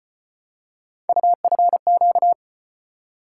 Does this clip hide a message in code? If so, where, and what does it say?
Yes: Morse code from 1.1 to 2.3 s, heard as a keyed tone, about 35 wpm, 700 Hz. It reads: UFQ